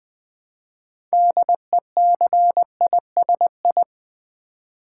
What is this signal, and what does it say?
Morse code, heard as a keyed tone, about 20 wpm, 700 Hz. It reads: DECISI
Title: Build A Word - 3 Letters At A Time - 8-Letter Words - 20wpm